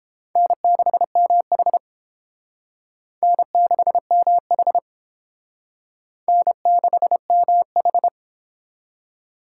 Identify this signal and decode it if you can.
Morse code, heard as a keyed tone, about 29 wpm, 700 Hz. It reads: N6MH N6MH N6MH